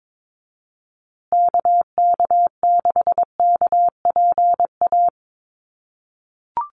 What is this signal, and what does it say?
Morse code, heard as a keyed tone, about 22 wpm, 700 Hz. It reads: KK6KPA
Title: USA Call Signs Repeated 3x 22wpm